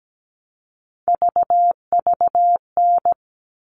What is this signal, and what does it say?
Morse code, heard as a keyed tone, about 17 wpm, 700 Hz. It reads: VVN
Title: All Cut Numbers - 3-Digit 17wpm